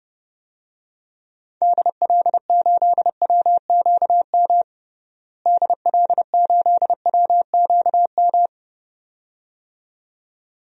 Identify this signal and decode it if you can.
Morse code, heard as a keyed tone, about 29 wpm, 700 Hz. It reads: DL8WQM DL8WQM
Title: DXCC-100 Call Signs - 6 Characters - Repeated 2x - 30wpm